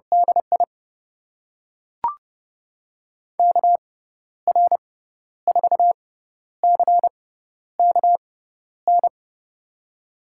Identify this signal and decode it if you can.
Morse code, heard as a keyed tone, about 30 wpm, spaced as for 12 wpm, 700 Hz. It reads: DI KR4CKN
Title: USA Call Signs - 6 Characters - 6x Character Spacing - 30wpm